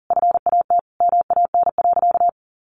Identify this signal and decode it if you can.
Morse code, heard as a keyed tone, about 40 wpm, 700 Hz. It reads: FAT MAN.
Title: Sentences from Top 1100 Words 40wpm